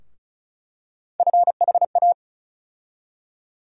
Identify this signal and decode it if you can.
Morse code, heard as a keyed tone, about 36 wpm, 700 Hz. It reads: FHA
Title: USA Call Sign Postfixes - 3 Letters 35wpm